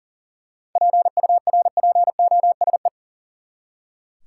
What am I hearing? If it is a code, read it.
Morse code, heard as a keyed tone, about 40 wpm, 700 Hz. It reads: PURPOSE
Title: Seven Letter Words - 40wpm